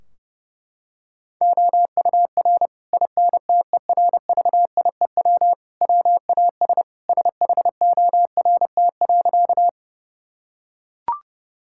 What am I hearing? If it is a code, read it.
Morse code, heard as a keyed tone, about 30 wpm, 700 Hz. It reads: OUR INTERVIEW WAS SHORT.
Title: Sentences from Top 1600 Words 30wpm